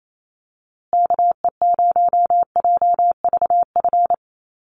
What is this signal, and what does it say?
Morse code, heard as a keyed tone, about 28 wpm, 700 Hz. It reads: KE0JVF
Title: USA Call Signs Repeated 2x 28wpm